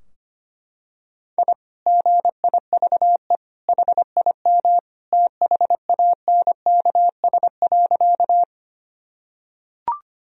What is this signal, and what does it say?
Morse code, heard as a keyed tone, about 25 wpm, 700 Hz. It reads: I GIVE HIM THANKS.
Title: Sentences from Top 1000 Words - Review Part 3 - 25wpm